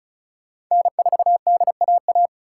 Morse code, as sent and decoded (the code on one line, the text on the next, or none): -. ....- -.. .- .-
N4DAA